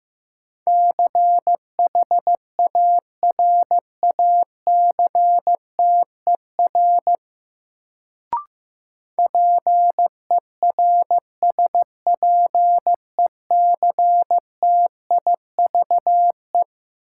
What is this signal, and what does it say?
Morse code, heard as a keyed tone, about 15 wpm, 700 Hz. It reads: CHARACTER PERSPECTIVE
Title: Sentences from Top 1200 Words - New Word Drill - 15wpm